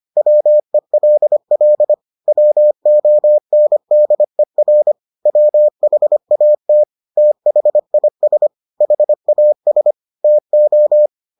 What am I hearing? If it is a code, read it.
Morse code, heard as a keyed tone, about 25 wpm, 600 Hz. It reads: WELL WONDER WHAT THIS HAS TO